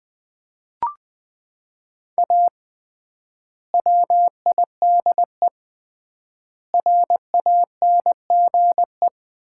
Morse code, heard as a keyed tone, about 20 wpm, 700 Hz. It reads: A WIDE RANGE
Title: 3x Word-Spacing - Sets of 4 Words - 20wpm